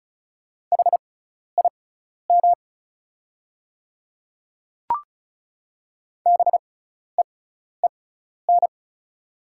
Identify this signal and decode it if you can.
Morse code, heard as a keyed tone, about 34 wpm, spaced as for 12 wpm, 700 Hz. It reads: HIM BEEN
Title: 6x Character Spacing - Top 200 Words - 35wpm